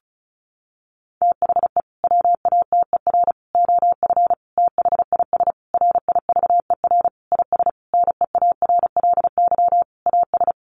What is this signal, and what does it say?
Morse code, heard as a keyed tone, about 34 wpm, 700 Hz. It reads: THE WATER OF THIS RIVER IS NEARLY AS
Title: Sentences from Top 1500 Words 35wpm